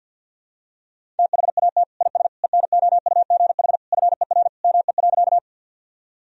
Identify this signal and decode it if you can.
Morse code, heard as a keyed tone, about 52 wpm, 700 Hz. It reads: THAT IS ENOUGH FER ME.
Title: Sentences from Top 400 Words and Abbreviations 50wpm